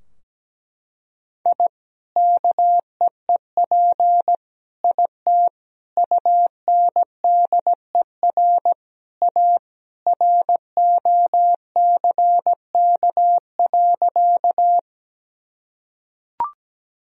Morse code, heard as a keyed tone, about 17 wpm, 700 Hz. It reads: I KEEP IT UNDER A ROCK.
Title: Sentences from Top 1000 Words - Review Part 2 - 17wpm